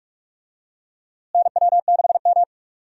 Morse code, heard as a keyed tone, about 43 wpm, 700 Hz. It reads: NW6M
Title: USA Call Signs - 4 Characters - Repeated 3x - 45wpm